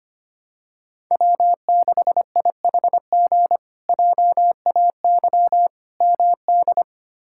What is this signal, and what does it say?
Morse code, heard as a keyed tone, about 25 wpm, 700 Hz. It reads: W6IHG JAY MD
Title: K1USN SST QSO Practice - 25wpm